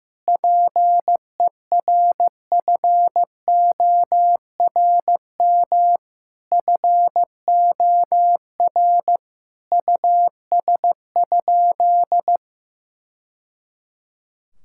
Morse code, heard as a keyed tone, about 16 wpm, 700 Hz. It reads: PERFORM FOR US?
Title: Sentences from Top 1000 Words - Review Part 5 - 15wpm